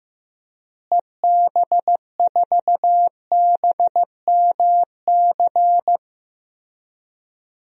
Morse code, heard as a keyed tone, about 15 wpm, 700 Hz. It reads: EB4BMC
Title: DXCC-100 Call Signs - 6 Characters - Repeated 3x - 15wpm